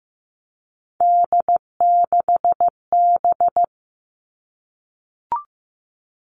Morse code, heard as a keyed tone, about 15 wpm, 700 Hz. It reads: D6B
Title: Three Challenging Letter-Number Combinations 15wpm